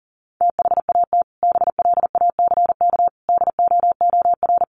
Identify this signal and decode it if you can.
Morse code, heard as a keyed tone, about 40 wpm, 700 Hz. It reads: THAT BLACK DOOR
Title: Sentences from Top 1400 Words 40wpm